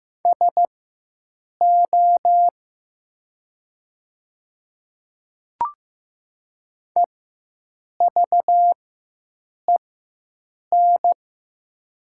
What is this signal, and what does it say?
Morse code, heard as a keyed tone, about 15 wpm, spaced as for 7 wpm, 700 Hz. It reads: SO EVEN
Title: 4x Character Spacing - Top 200 Words - 15wpm